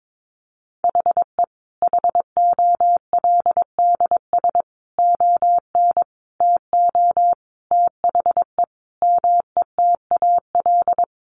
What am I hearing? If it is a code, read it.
Morse code, heard as a keyed tone, about 22 wpm, 700 Hz. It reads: HE HOLDS ON TO THE METAL
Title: Additional Easy Phrases 22wpm